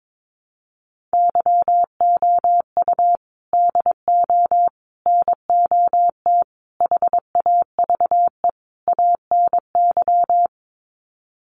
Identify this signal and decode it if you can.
Morse code, heard as a keyed tone, about 22 wpm, 700 Hz. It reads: YOU DO NOT HAVE ANY
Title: Sets of 5 Words Encore 22wpm